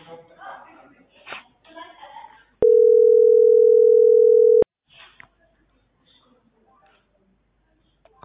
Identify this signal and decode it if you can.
Ringback tone (North America), 440 + 480 Hz, with speech.
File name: outgoing recordings